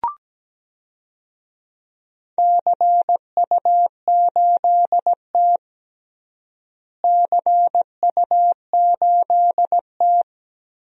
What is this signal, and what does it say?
Morse code, heard as a keyed tone, about 17 wpm, 700 Hz. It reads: CU8T CU8T
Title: DXCC-100 Call Signs - 4 Characters - Repeated 3x - 17wpm